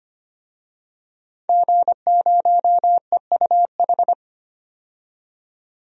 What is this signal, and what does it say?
Morse code, heard as a keyed tone, about 25 wpm, 700 Hz. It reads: G0EUH
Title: DXCC-100 Call Signs - 5 Characters - 25wpm